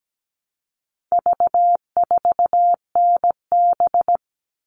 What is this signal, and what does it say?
Morse code, heard as a keyed tone, about 17 wpm, 700 Hz. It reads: V4NB
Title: All Cut Numbers - 4-Digit 17wpm